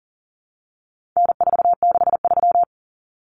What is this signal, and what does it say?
Morse code, heard as a keyed tone, about 40 wpm, 700 Hz. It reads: N463
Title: Common Cut Numbers - 4-Digit 40wpm